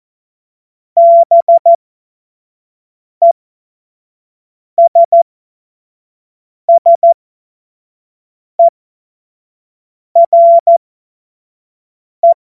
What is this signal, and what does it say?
Morse code, heard as a keyed tone, about 14 wpm, spaced as for 5 wpm, 680 Hz. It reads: BESSERE
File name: Morsekurs / Witze / Witz 06. September 2022 25 F70